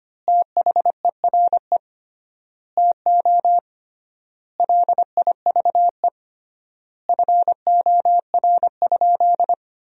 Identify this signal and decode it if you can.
Morse code, heard as a keyed tone, about 25 wpm, 700 Hz. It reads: THERE TO LIVE FOR?
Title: 3x Word-Spacing - Sentences from Top 200 Words - 25wpm